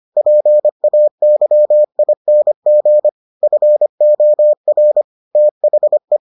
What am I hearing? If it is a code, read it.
Morse code, heard as a keyed tone, about 25 wpm, 600 Hz. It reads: PAYING FOR THE